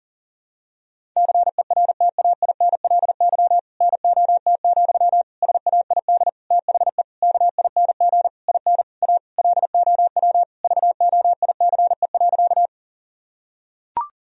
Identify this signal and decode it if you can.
Morse code, heard as a keyed tone, about 40 wpm, 700 Hz. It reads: CERTAINLY NOT, SAID THE KING IN A LOW VOICE.
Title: Sentences from Top 1000 Words - Review Part 1 - 40wpm